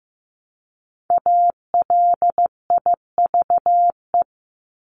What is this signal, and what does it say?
Morse code, heard as a keyed tone, about 15 wpm, 700 Hz. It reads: ALIVE
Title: Top 1751-2000 Words 15wpm